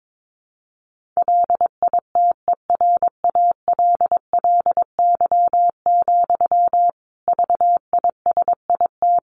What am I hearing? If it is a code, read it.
Morse code, heard as a keyed tone, about 22 wpm, 700 Hz. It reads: LITERALLY, VISIT
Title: Sentences from Top 1400 Words 22wpm